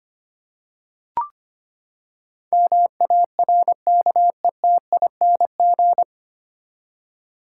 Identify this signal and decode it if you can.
Morse code, heard as a keyed tone, about 25 wpm, 700 Hz. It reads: MARKETING
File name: Nine Letter Words - 25wpm